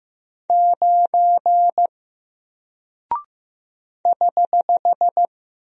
Morse code, heard as a keyed tone, about 15 wpm, 700 Hz. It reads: 9 <HH>
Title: Single Character (Letter:Number:Punctuation:Prosign) Alt 15wpm